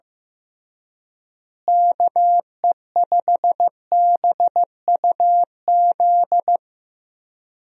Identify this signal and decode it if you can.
Morse code, heard as a keyed tone, about 15 wpm, 700 Hz. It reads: KE5BUZ
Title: USA Call Signs - 6 Characters - Repeated 2x - 15wpm